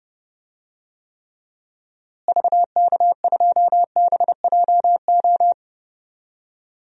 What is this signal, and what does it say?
Morse code, heard as a keyed tone, about 30 wpm, 700 Hz. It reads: VK2BJO
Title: DXCC-100 Call Signs - 6 Characters - Repeated 2x - 30wpm